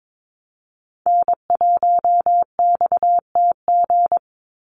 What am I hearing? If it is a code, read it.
Morse code, heard as a keyed tone, about 22 wpm, 700 Hz. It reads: N1XTG
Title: USA Call Signs 22wpm